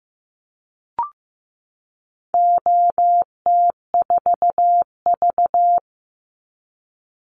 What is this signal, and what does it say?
Morse code, heard as a keyed tone, about 15 wpm, 700 Hz. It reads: OT4V
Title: Call Signs Repeated 2x - 15wpm